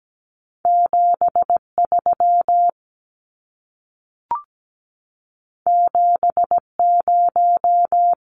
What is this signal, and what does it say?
Morse code, heard as a keyed tone, about 17 wpm, 700 Hz. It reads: 73 70
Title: Numbers - 2-Digit 17wpm